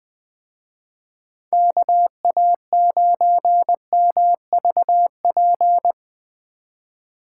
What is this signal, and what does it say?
Morse code, heard as a keyed tone, about 20 wpm, 700 Hz. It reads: KA9MVP
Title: Speed Racing - USA 2x3 Call Signs 17wpm